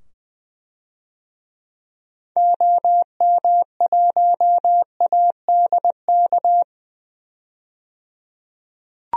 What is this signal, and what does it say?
Morse code, heard as a keyed tone, about 20 wpm, 700 Hz. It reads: OM1ADK
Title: DXCC-100 Call Signs - 6 Characters - 20wpm